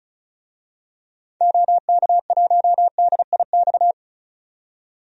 Morse code, heard as a keyed tone, about 36 wpm, 700 Hz. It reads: OK1DIX
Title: DXCC-100 Call Signs - 6 Characters - 35wpm